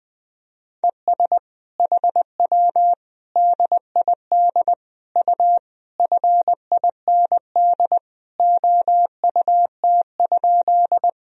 Morse code, heard as a keyed tone, about 21 wpm, 700 Hz. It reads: ES HW DID U FIND OUT?
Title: Sentences from Top 100 Words and Abbreviations 20wpm